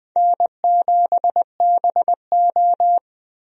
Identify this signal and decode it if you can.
Morse code, heard as a keyed tone, about 21 wpm, 700 Hz. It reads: N7BO
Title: USA Call Signs - 4 Characters - Repeated 2x - 20wpm